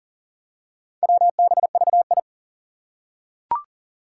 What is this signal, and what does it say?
Morse code, heard as a keyed tone, about 38 wpm, 700 Hz. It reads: WBVI
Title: Four Challenging Letter-Number Combinations 40wpm